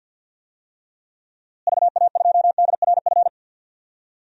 Morse code, heard as a keyed tone, about 51 wpm, 700 Hz. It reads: VA2DRF
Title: DXCC-100 Call Signs - 6 Characters - Repeated 3x - 50wpm